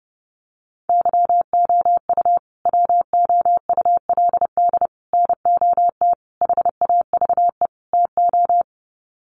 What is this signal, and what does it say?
Morse code, heard as a keyed tone, about 30 wpm, 700 Hz. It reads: YOU WOULD NOT HAVE TO
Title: Sets of 5 Words Encore 30wpm